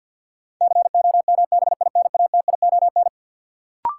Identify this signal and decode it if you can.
Morse code, heard as a keyed tone, about 52 wpm, 700 Hz. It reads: COMBINATION
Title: Sentences from Top 1300 Words - New Word Drill Alt - 50wpm